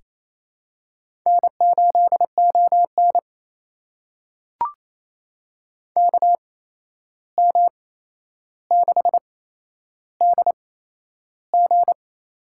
Morse code, heard as a keyed tone, about 28 wpm, spaced as for 11 wpm, 700 Hz. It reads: N8ON KM6DG
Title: USA Call Signs - 8x Character Spacing - 28wpm